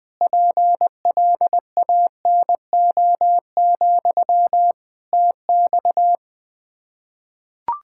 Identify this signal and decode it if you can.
Morse code, heard as a keyed tone, about 20 wpm, 700 Hz. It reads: PLANO, TX
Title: K1USN SST QSO Practice 20wpm